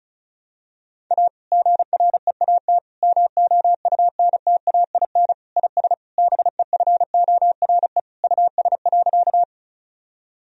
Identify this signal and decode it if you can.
Morse code, heard as a keyed tone, about 34 wpm, 700 Hz. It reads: A GREAT MOUNTAIN IS BEFORE US.
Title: Sentences from Top 1000 Words - Review Part 1 - 35wpm